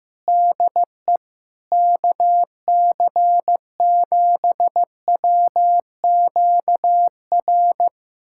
Morse code, heard as a keyed tone, about 15 wpm, 700 Hz. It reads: DE KC7WQR
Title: USA Callsign DE Callsign 15wpm